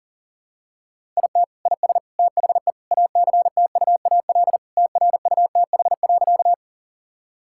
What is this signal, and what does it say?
Morse code, heard as a keyed tone, about 37 wpm, 700 Hz. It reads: IT IS THE ACTUAL TRUTH.
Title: Sentences from Top 1000 Words - Review Part 5 - 40wpm